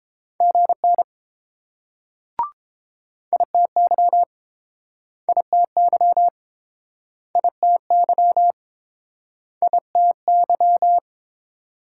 Morse code, heard as a keyed tone, about 30 wpm, 700 Hz. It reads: GN ITY ITY ITY ITY